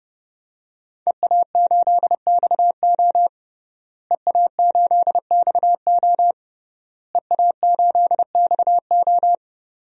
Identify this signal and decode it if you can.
Morse code, heard as a keyed tone, about 30 wpm, 700 Hz. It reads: EA8XO EA8XO EA8XO
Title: DXCC-100 Call Signs - 5 Characters - Repeated 3x - 30wpm